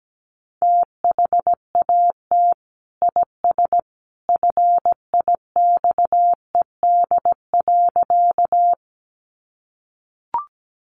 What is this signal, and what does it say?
Morse code, heard as a keyed tone, about 17 wpm, 700 Hz. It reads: THAT IS FIXED.
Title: Sentences from Top 1100 Words 17wpm